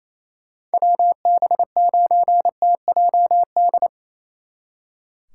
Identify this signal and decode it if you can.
Morse code, heard as a keyed tone, about 34 wpm, 700 Hz. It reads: WB9TJD
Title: Speed Racing - USA Call Signs 28wpm